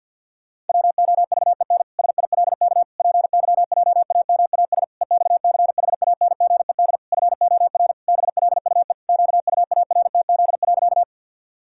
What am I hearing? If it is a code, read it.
Morse code, heard as a keyed tone, about 51 wpm, 700 Hz. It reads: WOVEN SILK PYJAMAS EXCHANGED FOR BLUE QUARTZ.
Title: Pangram Challenge v2 - 50wpm